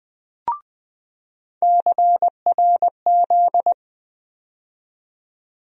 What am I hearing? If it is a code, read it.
Morse code, heard as a keyed tone, about 20 wpm, 700 Hz. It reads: CRZ
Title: USA Call Sign Postfixes - 3 Letters 20wpm